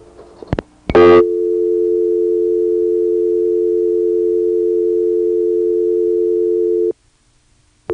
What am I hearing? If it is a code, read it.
Dial tone (North America), 350 + 440 Hz.
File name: phonetrips / HarrisonbgVA1HQ